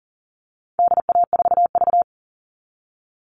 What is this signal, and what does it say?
Morse code, heard as a keyed tone, about 40 wpm, 700 Hz. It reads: DA4V
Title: All Cut Numbers - 4-Digit 40wpm